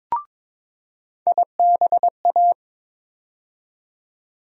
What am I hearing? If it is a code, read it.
Morse code, heard as a keyed tone, about 22 wpm, 700 Hz. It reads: IBA
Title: USA Call Sign Postfixes - 3 Letters 22wpm